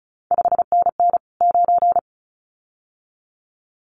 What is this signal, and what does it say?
Morse code, heard as a keyed tone, about 36 wpm, 700 Hz. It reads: 5NN 9
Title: CQ WW DX Contest Exchange - 35wpm